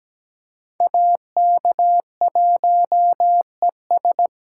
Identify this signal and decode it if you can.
Morse code, heard as a keyed tone, about 20 wpm, 700 Hz. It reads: AK1ES
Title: Speed Racing - USA 2x2 Call Signs 17wpm